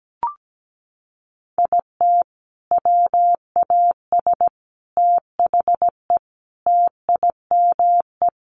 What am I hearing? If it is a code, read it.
Morse code, heard as a keyed tone, about 17 wpm, 700 Hz. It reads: IT WAS THE TIME
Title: Sentences from Top 700 Words 17wpm